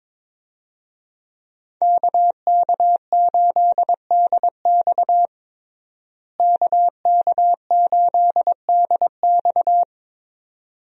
Morse code, heard as a keyed tone, about 23 wpm, 700 Hz. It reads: KK8DX KK8DX
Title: USA Call Signs - 5 Characters - Repeated 2x - 22wpm